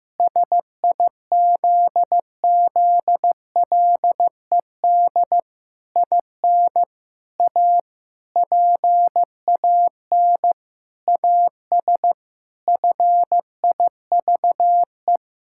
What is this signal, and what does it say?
Morse code, heard as a keyed tone, about 15 wpm, 700 Hz. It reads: SIZZLED IN A PAN AS FIVE
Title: Pangram Challenge v2 - 15wpm